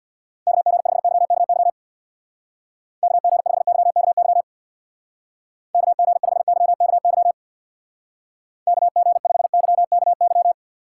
Morse code, heard as a keyed tone, about 66 wpm, 700 Hz. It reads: KG5YKY KG5YKY KG5YKY KG5YKY